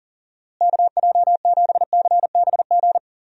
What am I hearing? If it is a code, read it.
Morse code, heard as a keyed tone, about 38 wpm, 700 Hz. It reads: KJ7CBG
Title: USA Call Signs - 6 Characters - Repeated 2x - 40wpm